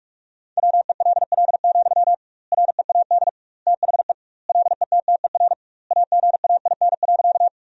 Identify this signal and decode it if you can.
Morse code, heard as a keyed tone, about 46 wpm, 700 Hz. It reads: WELL, READ THE LETTER AGAIN.
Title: 2x Word-Spacing - Sentences from Top 300 Words - 45wpm